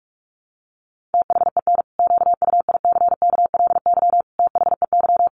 Morse code, heard as a keyed tone, about 44 wpm, 700 Hz. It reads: THEN QUICKLY THEY
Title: Sentences from Top 700 Words 45wpm